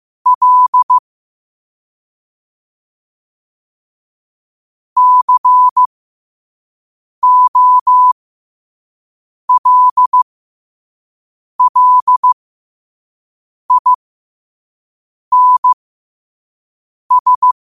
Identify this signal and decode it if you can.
Morse code, heard as a keyed tone, about 15 wpm, spaced as for 5 wpm, 1000 Hz. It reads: L COLLINS
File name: qotdfiles / 5wpm